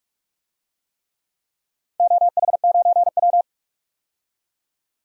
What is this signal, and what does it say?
Morse code, heard as a keyed tone, about 43 wpm, 700 Hz. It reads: OH9W
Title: DXCC-100 Call Signs - 4 Characters - 45wpm